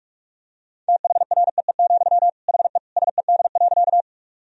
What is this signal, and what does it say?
Morse code, heard as a keyed tone, about 47 wpm, 700 Hz. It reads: THREE, HE SED.
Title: Sentences from Top 200 Words and Abbreviations 45wpm